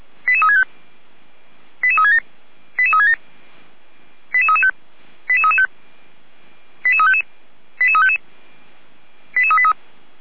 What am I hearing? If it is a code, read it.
Five-tone selective call (ZVEI), heard as quick runs of five tones, tones of about 70 ms. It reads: 88365 88367 88367 88335 88335 88355 88355 88383